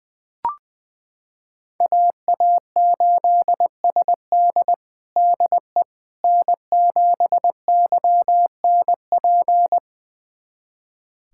Morse code, heard as a keyed tone, about 20 wpm, 700 Hz. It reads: AA8SD DE N7YNP
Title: USA Callsign DE Callsign - 5 Characters - 20wpm